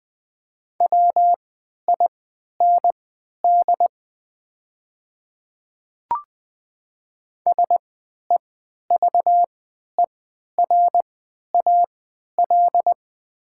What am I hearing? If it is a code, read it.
Morse code, heard as a keyed tone, about 20 wpm, spaced as for 11 wpm, 700 Hz. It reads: WIND SEVERAL
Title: 3x Character Spacing - Top 500 Words - 20wpm